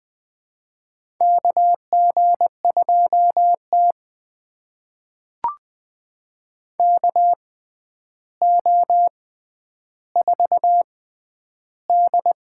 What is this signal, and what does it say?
Morse code, heard as a keyed tone, about 20 wpm, spaced as for 10 wpm, 700 Hz. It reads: KG2T KO4D
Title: USA Call Signs - 6x Character Spacing - 20wpm